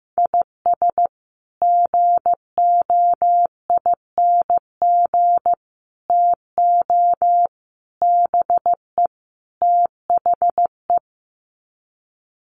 Morse code, heard as a keyed tone, about 15 wpm, 700 Hz. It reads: IS GOING TO BE THE
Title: Sets of 5 Words 15wpm